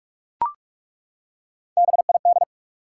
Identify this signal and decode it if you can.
Morse code, heard as a keyed tone, about 42 wpm, 700 Hz. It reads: BID
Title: Three Letter Words - 45wpm